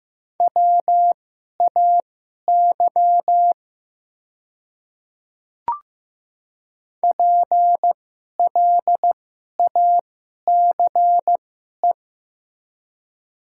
Morse code, heard as a keyed tone, about 15 wpm, spaced as for 11 wpm, 700 Hz. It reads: WAY PLACE
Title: 2x Character Spacing - Top 500 Words - 15wpm